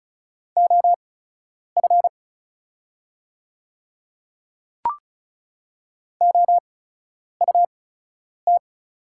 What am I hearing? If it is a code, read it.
Morse code, heard as a keyed tone, about 36 wpm, spaced as for 10 wpm, 700 Hz. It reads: OF OUT
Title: 8x Character Spacing - Top 100 Words - 35wpm